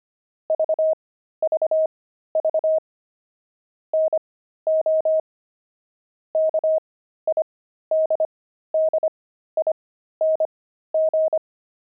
Morse code, heard as a keyed tone, about 25 wpm, spaced as for 13 wpm, 630 Hz. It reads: VVV NO KIDDING
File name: pangram-fs-13-cs-25-kidding